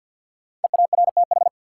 Morse code, heard as a keyed tone, about 51 wpm, 700 Hz. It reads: EARTH
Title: Sentences from Top 200 Words - New Word Drill - 50wpm